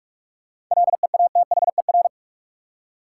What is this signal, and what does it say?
Morse code, heard as a keyed tone, about 46 wpm, 700 Hz. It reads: LEATHER
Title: Seven Letter Words - 45wpm